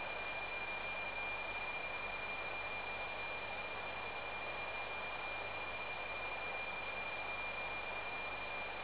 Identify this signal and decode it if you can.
PSK31, about 31.25 baud.